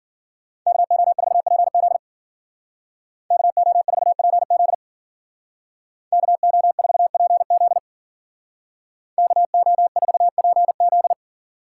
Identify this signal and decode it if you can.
Morse code, heard as a keyed tone, about 53 wpm, 700 Hz. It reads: KO4PZ KO4PZ KO4PZ KO4PZ